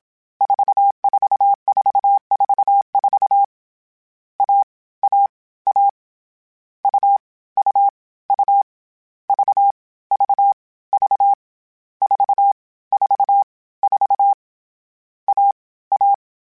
Morse code, heard as a keyed tone, about 27 wpm, 800 Hz. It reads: 44444 A A A U U U V V V 4 4 4 A A
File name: CW Code Groups-mp3s / Sets mp3s / AUV AUV4 sp @25